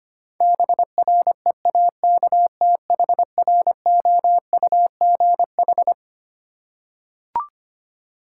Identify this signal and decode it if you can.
Morse code, heard as a keyed tone, about 25 wpm, 700 Hz. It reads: BREAKTHROUGH
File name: Twelve Letter Words - 25wpm